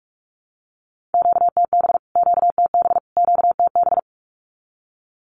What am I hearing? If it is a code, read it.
Morse code, heard as a keyed tone, about 44 wpm, 700 Hz. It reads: QTB QTB QTB
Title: Q-Codes Repeated 3x 45wpm